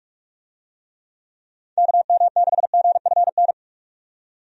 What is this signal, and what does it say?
Morse code, heard as a keyed tone, about 46 wpm, 700 Hz. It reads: KM6GFN
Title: USA Call Signs - 6 Characters - 45wpm